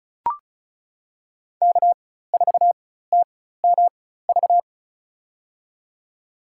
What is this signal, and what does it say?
Morse code, heard as a keyed tone, about 36 wpm, spaced as for 16 wpm, 700 Hz. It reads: K4TMV
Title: USA Call Signs - 5 Characters - 4x Character Spacing - 35wpm